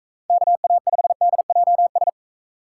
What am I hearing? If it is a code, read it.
Morse code, heard as a keyed tone, about 42 wpm, 700 Hz. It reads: KA5DJS